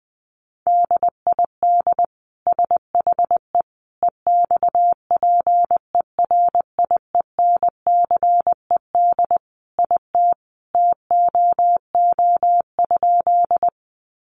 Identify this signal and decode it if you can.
Morse code, heard as a keyed tone, about 20 wpm, 700 Hz. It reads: DID SHE EXPERIENCED IT TOO?
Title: Sentences from Top 1100 Words 20wpm